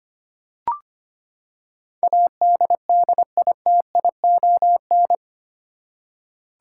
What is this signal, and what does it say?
Morse code, heard as a keyed tone, about 25 wpm, 700 Hz. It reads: ADDITION
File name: Eight Letter Words - 25wpm